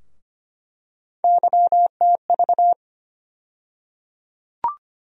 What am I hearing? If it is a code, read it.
Morse code, heard as a keyed tone, about 25 wpm, 700 Hz. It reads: YTV
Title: USA Call Sign Postfixes - 3 Letters 25wpm